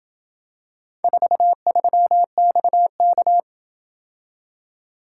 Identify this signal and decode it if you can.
Morse code, heard as a keyed tone, about 26 wpm, 700 Hz. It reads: 43XK